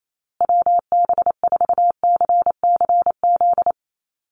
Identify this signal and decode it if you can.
Morse code, heard as a keyed tone, about 28 wpm, 700 Hz. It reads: WB4CCZ
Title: USA Call Signs Repeated 2x 28wpm